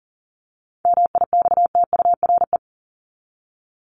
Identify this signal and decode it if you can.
Morse code, heard as a keyed tone, about 40 wpm, 700 Hz. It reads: MIXTURE
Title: Top 1251-1500 Words 40wpm